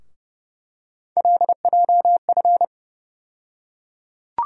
Morse code, heard as a keyed tone, about 31 wpm, 700 Hz. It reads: LJF
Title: USA Call Sign Postfixes - 3 Letters 30wpm